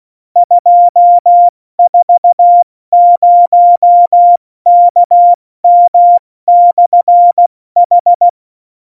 Morse code, heard as a keyed tone, about 16 wpm, spaced as for 14 wpm, 700 Hz. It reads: 240KM/H